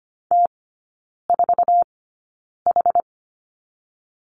Morse code, heard as a keyed tone, about 25 wpm, spaced as for 9 wpm, 690 Hz. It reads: T4H
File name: Lesson 14e - Number 2 - Three Characters 25wpm